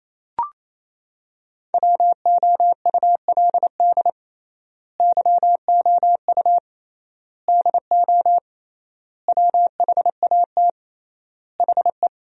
Morse code, heard as a keyed tone, about 28 wpm, 700 Hz. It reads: WOULD YOU DO WHAT HE
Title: 3x Word-Spacing - Sentences from Top 100 Words - 28wpm